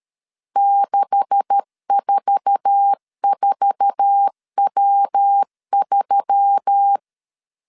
Morse code, heard as a keyed tone, about 13 wpm, 800 Hz. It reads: 644W3
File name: CW / DET / 1 9 1500C S B 12wpm 3819